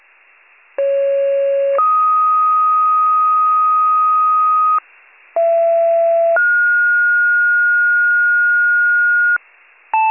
Two-tone page, heard as a one-second tone, then a three-second tone, about 550 then 1250 Hz, about 670 then 1430 Hz, with noise. It is heard from 0.8 to 9.4 s.